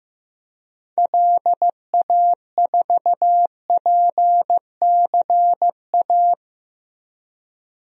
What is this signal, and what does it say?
Morse code, heard as a keyed tone, about 15 wpm, 700 Hz. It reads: LA4PCA
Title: DXCC-100 Call Signs - 6 Characters - 15wpm